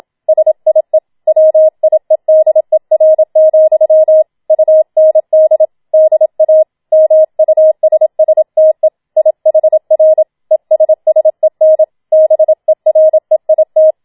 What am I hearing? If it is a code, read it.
Morse code, heard as a keyed tone, about 27 wpm, 630 Hz. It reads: SIE WIEDER, UND DA MUSSTE IHR ESSEN BEREIT
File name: Schneewittchen Teil2 27wpm